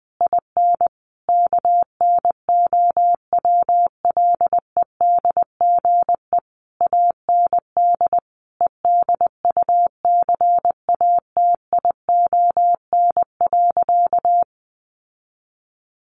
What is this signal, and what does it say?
Morse code, heard as a keyed tone, about 20 wpm, 700 Hz. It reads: IN KNOWLEDGE AND EDUCATION.
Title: Sentences from Top 1500 Words 20wpm